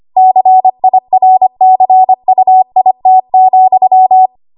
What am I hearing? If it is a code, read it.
Morse code, heard as a keyed tone, about 25 wpm, 750 Hz. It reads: CIRCUIT,
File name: Archive / 25 WPM / 25WPM